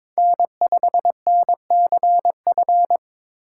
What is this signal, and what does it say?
Morse code, heard as a keyed tone, about 22 wpm, 700 Hz. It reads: N5NCF
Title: USA Call Signs - 5 Characters - 4x Character Spacing - 22wpm